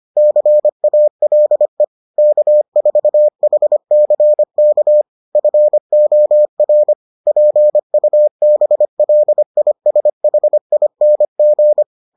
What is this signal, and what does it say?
Morse code, heard as a keyed tone, about 25 wpm, 600 Hz. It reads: CALE K4HCK FOR PUBLISHING